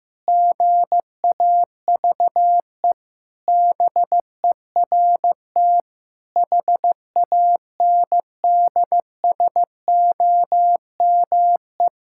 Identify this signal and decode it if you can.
Morse code, heard as a keyed tone, about 15 wpm, 700 Hz. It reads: GAVE BERT HANDSOME
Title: Pangram Challenge v2 - 15wpm